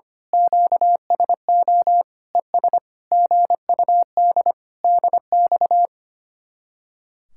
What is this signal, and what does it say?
Morse code, heard as a keyed tone, about 24 wpm, 700 Hz. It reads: QSO ES GUD DX
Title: Speed-Racing Common QSO Phrases - 25wpm